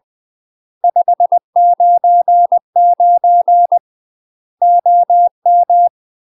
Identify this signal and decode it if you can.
Morse code, heard as a keyed tone, about 20 wpm, 700 Hz. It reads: 599 OM
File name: UK MASTER QSOs TRAINING / 20WPM / UK MASTER QSOs 0007